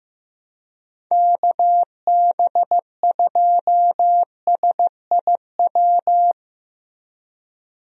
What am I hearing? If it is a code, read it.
Morse code, heard as a keyed tone, about 15 wpm, 700 Hz. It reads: KB2SIW
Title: USA Call Signs - 6 Characters - Repeated 3x - 15wpm